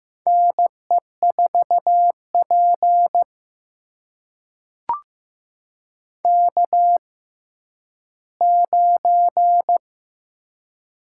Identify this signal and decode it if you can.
Morse code, heard as a keyed tone, about 15 wpm, spaced as for 10 wpm, 700 Hz. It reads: NE4P K9
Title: USA Call Signs - 4 Characters - 6x Character Spacing - 15wpm